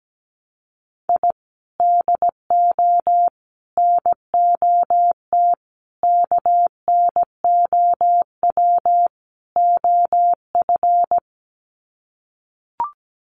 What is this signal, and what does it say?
Morse code, heard as a keyed tone, about 17 wpm, 700 Hz. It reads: I DO NOT KNOW OF
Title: Sets of 5 Words Encore2 17wpm